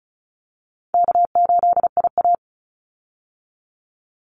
Morse code, heard as a keyed tone, about 34 wpm, 700 Hz. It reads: K8IA
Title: Easy Call Signs Repeated 2x - 35wpm